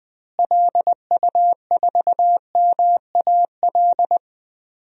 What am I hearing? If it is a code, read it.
Morse code, heard as a keyed tone, about 20 wpm, 700 Hz. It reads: LU4MAL
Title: DXCC-100 Call Signs - 6 Characters - Repeated 3x - 20wpm